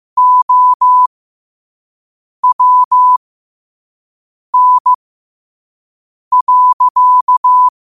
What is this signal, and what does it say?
Morse code, heard as a keyed tone, about 15 wpm, spaced as for 5 wpm, 1000 Hz. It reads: OWN.
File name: 5wpm